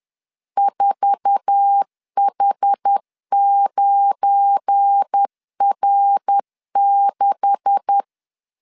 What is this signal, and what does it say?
Morse code, heard as a keyed tone, about 10 wpm, 800 Hz. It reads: 4H9R6
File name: CW / DET / 1 9 1500C S B 10wpm 4439